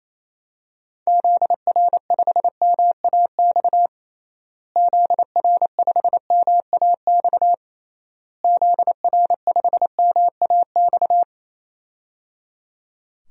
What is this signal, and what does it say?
Morse code, heard as a keyed tone, about 28 wpm, 700 Hz. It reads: ZR5MAX ZR5MAX ZR5MAX
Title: DXCC-100 Call Signs - 6 Characters - Repeated 3x - 28wpm